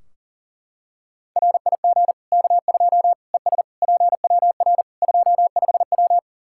Morse code, heard as a keyed tone, about 37 wpm, 700 Hz. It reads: RIG K2 ES PWR 25W
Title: Practice QSO - Exchange 2 - Info - 40wpm